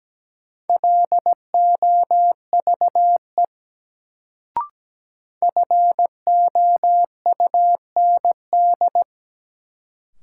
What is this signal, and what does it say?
Morse code, heard as a keyed tone, about 17 wpm, 700 Hz. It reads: LOVE FOUND
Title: Sets of 5 Words Encore - New Word Drill - 17wpm